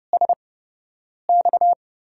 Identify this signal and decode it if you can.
Morse code, heard as a keyed tone, about 30 wpm, spaced as for 8 wpm, 700 Hz. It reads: SX
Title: USA Call Signs - 8x Character Spacing - 30wpm